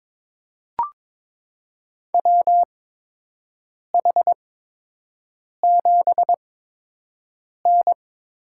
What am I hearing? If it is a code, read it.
Morse code, heard as a keyed tone, about 22 wpm, spaced as for 6 wpm, 700 Hz. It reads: WH7N
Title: USA Call Signs - 5 Characters - 8x Character Spacing - 22wpm